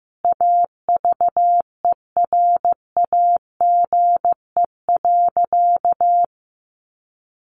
Morse code, heard as a keyed tone, about 15 wpm, 700 Hz. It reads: AVERAGE.
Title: Sentences from Top 1400 Words 15wpm